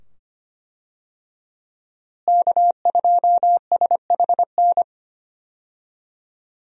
Morse code, heard as a keyed tone, about 25 wpm, 700 Hz. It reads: K2SHN
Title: USA Call Signs - 5 Characters - Repeated 2x - 25wpm